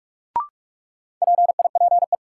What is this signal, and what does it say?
Morse code, heard as a keyed tone, about 46 wpm, 700 Hz. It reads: PIPE
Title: Sentences from Top 1500 Words - New Word Drill - 45wpm